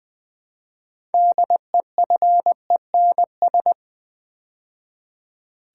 Morse code, heard as a keyed tone, about 20 wpm, 700 Hz. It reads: DEFENS
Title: Build A Word - 3 Letters At A Time - 9-Letter Words - 20wpm